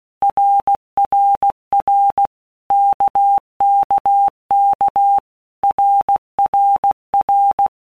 Morse code, heard as a keyed tone, about 16 wpm, 800 Hz. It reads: RRR KKK RRR
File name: Basic Letter Groups By Blocks / Groups@15 / RKPX@15